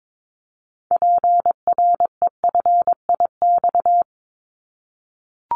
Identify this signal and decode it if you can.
Morse code, heard as a keyed tone, about 22 wpm, 700 Hz. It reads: PREFIX
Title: Callsign Prefixes for Top 25 DXCC - 22wpm